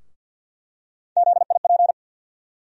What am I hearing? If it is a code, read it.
Morse code, heard as a keyed tone, about 50 wpm, 700 Hz. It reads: ZIP
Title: USA Call Sign Postfixes - 3 Letters 50wpm